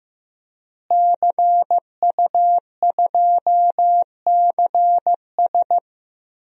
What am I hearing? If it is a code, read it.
Morse code, heard as a keyed tone, about 15 wpm, 700 Hz. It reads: CU2CS
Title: DXCC-100 Call Signs - 5 Characters - Repeated 3x - 15wpm